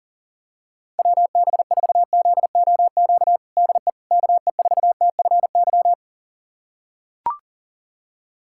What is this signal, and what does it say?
Morse code, heard as a keyed tone, about 40 wpm, 700 Hz. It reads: WB4ZOQ DE KE4TFY
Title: USA Callsign DE Callsign 40wpm